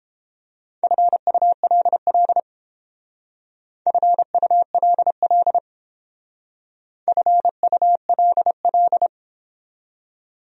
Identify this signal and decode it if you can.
Morse code, heard as a keyed tone, about 31 wpm, 700 Hz. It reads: FULL FULL FULL